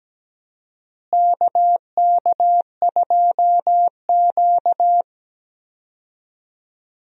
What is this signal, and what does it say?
Morse code, heard as a keyed tone, about 17 wpm, 700 Hz. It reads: KK2Q
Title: USA Call Signs - 4 Characters - Repeated 3x - 17wpm